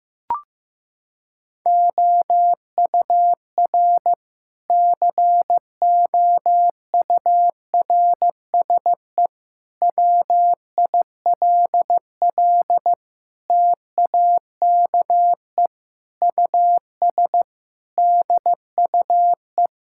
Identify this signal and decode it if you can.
Morse code, heard as a keyed tone, about 15 wpm, 700 Hz. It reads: OUR COURSE WILL TAKE US DUE
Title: Sentences from Top 1000 Words - Review Part 6 - 15wpm